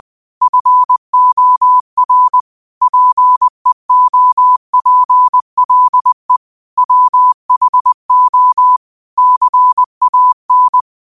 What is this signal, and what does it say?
Morse code, heard as a keyed tone, about 20 wpm, 1000 Hz. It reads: FOR PEOPLE WHO CAN
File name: qotdfiles / 20wpm